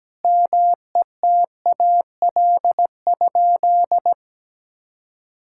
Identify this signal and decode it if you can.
Morse code, heard as a keyed tone, about 17 wpm, 700 Hz. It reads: METAL?
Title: Sentences from Top 1600 Words 17wpm